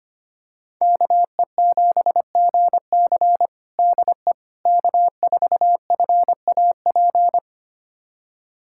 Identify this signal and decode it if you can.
Morse code, heard as a keyed tone, about 25 wpm, 700 Hz. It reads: KE7GC DE K4FAP
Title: USA Callsign DE Callsign 25wpm